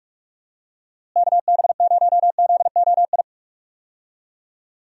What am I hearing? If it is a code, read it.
Morse code, heard as a keyed tone, about 41 wpm, 700 Hz. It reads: KB0ZOI
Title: USA Call Signs - 6 Characters - Repeated 2x - 45wpm